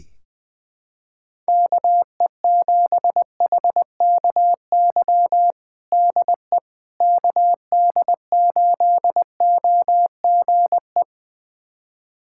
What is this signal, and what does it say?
Morse code, heard as a keyed tone, about 21 wpm, 700 Hz. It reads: KE7HKY DE KD8OGE
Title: USA Callsign DE Callsign - 6 Characters - 20wpm